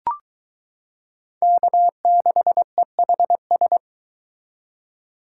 Morse code, heard as a keyed tone, about 22 wpm, 700 Hz. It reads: K6EHS